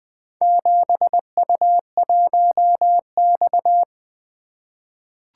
Morse code, heard as a keyed tone, about 20 wpm, 700 Hz. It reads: 7U1X
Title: Four Letter-Number Combinations - 20wpm